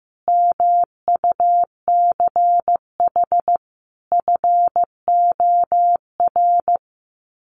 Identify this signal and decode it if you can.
Morse code, heard as a keyed tone, about 15 wpm, 700 Hz. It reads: MUCH FOR
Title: Sets of 5 Words 15wpm